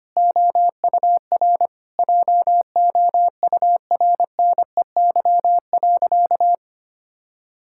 Morse code, heard as a keyed tone, about 25 wpm, 700 Hz. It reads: OUR JOURNEY.
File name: practice / 25wpm / Sentences with 9-Word Phrases - 25wpm